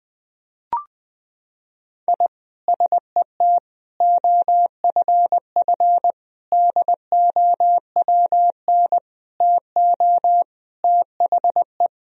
Morse code, heard as a keyed tone, about 20 wpm, 700 Hz. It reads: I SET OFF DOWN TO THE
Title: Sentences from Top 1000 Words - Review Part 1 - 20wpm